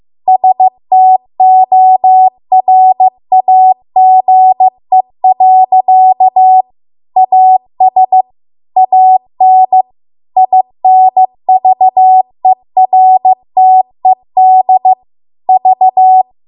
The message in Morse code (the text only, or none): STORAGE. AS AN INVERTED V